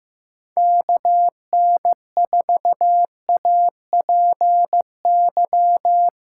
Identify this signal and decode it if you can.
Morse code, heard as a keyed tone, about 15 wpm, 700 Hz. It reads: KN4APY
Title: USA Call Signs - 6 Characters - 2x Character Spacing - 15wpm